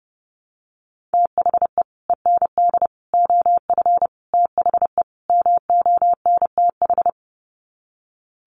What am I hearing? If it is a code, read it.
Morse code, heard as a keyed tone, about 30 wpm, 700 Hz. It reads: THE END OF THE MONTH
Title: Sets of 5 Words Encore 30wpm